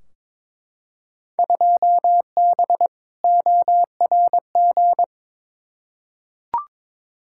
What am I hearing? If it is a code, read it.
Morse code, heard as a keyed tone, about 22 wpm, 700 Hz. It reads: 2B ORG
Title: Field Day Exchanges Encore - 22wpm